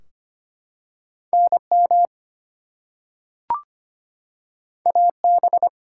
Morse code, heard as a keyed tone, about 25 wpm, 700 Hz. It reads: NM AB
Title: US States and Canadian Provinces 25wpm